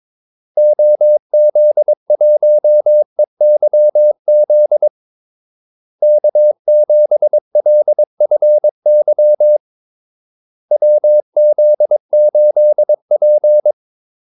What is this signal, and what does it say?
Morse code, heard as a keyed tone, about 22 wpm, 600 Hz. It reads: OZ1EYZ K7LFY WZ8P
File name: Callsigns-22wpm